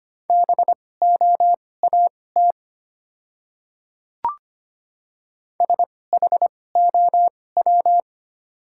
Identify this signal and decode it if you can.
Morse code, heard as a keyed tone, about 25 wpm, spaced as for 18 wpm, 690 Hz. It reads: BOAT SHOW
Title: 2x Character Spacing - Top 500 Words - 25wpm